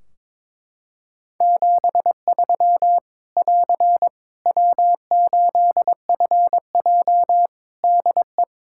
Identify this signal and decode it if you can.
Morse code, heard as a keyed tone, about 22 wpm, 700 Hz. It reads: 73 <AR> W8FJ DE
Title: Practice QSO - Exchange 4 - Ending - 22wpm